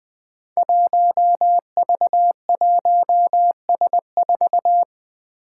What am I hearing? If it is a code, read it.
Morse code, heard as a keyed tone, about 21 wpm, 700 Hz. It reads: 1V1S4
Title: Five Challenging Letter-Number Combinations 20wpm